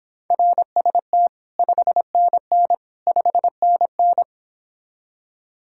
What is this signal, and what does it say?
Morse code, heard as a keyed tone, about 26 wpm, 700 Hz. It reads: RST 5NN 5NN